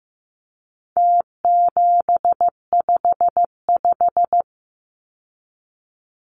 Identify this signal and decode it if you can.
Morse code, heard as a keyed tone, about 15 wpm, 700 Hz. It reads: T755
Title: Numbers - Contest Style - 4-Digit 15wpm